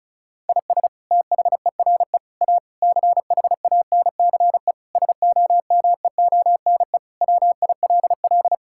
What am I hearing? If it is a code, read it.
Morse code, heard as a keyed tone, about 34 wpm, 700 Hz. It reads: IS THERE A CHANCE SOMEONE WILL
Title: Sentences from Top 1000 Words - Review Part 6 - 35wpm